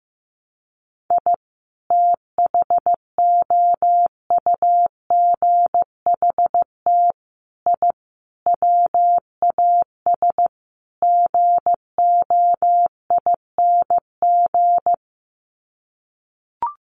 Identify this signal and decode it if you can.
Morse code, heard as a keyed tone, about 15 wpm, 700 Hz. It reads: I THOUGHT I WAS GOING
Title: Sets of 5 Words Encore2 15wpm